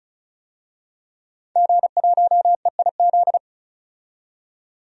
Morse code, heard as a keyed tone, about 34 wpm, 700 Hz. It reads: G1EIZ
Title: DXCC-100 Call Signs - 5 Characters - 35wpm